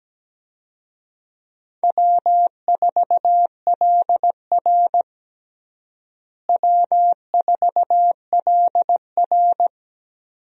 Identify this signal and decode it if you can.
Morse code, heard as a keyed tone, about 17 wpm, 700 Hz. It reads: W4LR W4LR
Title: USA Call Signs - 4 Characters - Repeated 2x - 17wpm